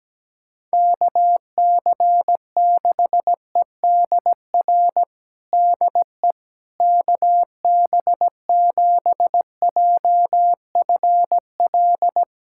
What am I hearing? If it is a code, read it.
Morse code, heard as a keyed tone, about 17 wpm, 700 Hz. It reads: KC6EDR DE KB7JFL
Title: USA Callsign DE Callsign 17wpm